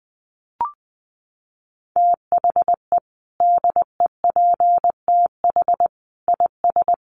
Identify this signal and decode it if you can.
Morse code, heard as a keyed tone, about 20 wpm, 700 Hz. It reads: THE DEPTH IS
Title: Sentences from Top 1200 Words 20wpm